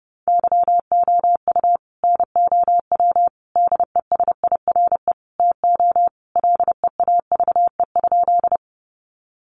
Morse code, heard as a keyed tone, about 28 wpm, 700 Hz. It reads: YOU NOW DESIRE TO LEAVE?
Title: Sentences from Top 1100 Words 30wpm